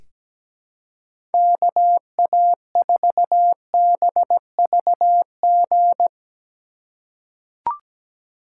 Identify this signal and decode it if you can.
Morse code, heard as a keyed tone, about 17 wpm, 700 Hz. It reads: KA4BVG
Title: K1USN SST QSO Practice - 17wpm